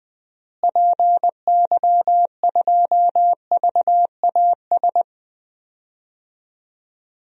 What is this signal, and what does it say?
Morse code, heard as a keyed tone, about 20 wpm, 700 Hz. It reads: PY2VAS
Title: DXCC-100 Call Signs - 6 Characters - Repeated 2x - 20wpm